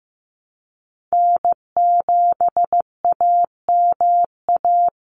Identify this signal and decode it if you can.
Morse code, heard as a keyed tone, about 15 wpm, 700 Hz. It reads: N7AMA
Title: NA SOTA Chaser Callsigns 15wpm